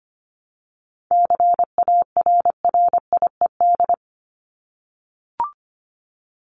Morse code, heard as a keyed tone, about 25 wpm, 700 Hz. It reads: CARRIED
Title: ED Suffix - Top 100 Words with Base Words 25wpm